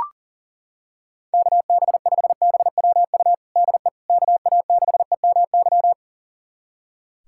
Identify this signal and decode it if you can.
Morse code, heard as a keyed tone, about 40 wpm, 690 Hz. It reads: KB5BWU DE KA6EMY
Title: USA Callsign DE Callsign - 6 Characters - 40wpm